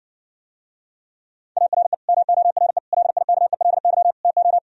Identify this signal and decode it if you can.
Morse code, heard as a keyed tone, about 66 wpm, 700 Hz. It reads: ARE MORE LIKELY TO